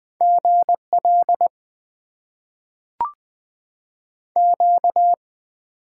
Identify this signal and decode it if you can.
Morse code, heard as a keyed tone, about 20 wpm, 700 Hz. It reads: GL Q
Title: Top 90 QSO Elements - 6x Character Spacing - 20wpm